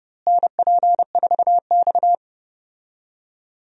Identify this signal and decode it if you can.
Morse code, heard as a keyed tone, about 30 wpm, 700 Hz. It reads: NP4X
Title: USA Call Signs - 4 Characters - Repeated 3x - 30wpm